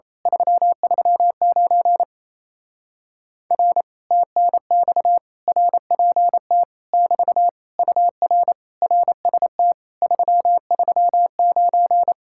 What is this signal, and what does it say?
Morse code, heard as a keyed tone, about 27 wpm, 700 Hz. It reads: 339 R TNX RPT <BT> UR RST 339
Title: Speed-Racing Common QSO Phrases - 28wpm